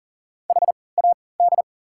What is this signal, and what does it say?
Morse code, heard as a keyed tone, about 40 wpm, spaced as for 23 wpm, 700 Hz. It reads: HAD
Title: 3x Character Spacing - Top 300 Words - 40wpm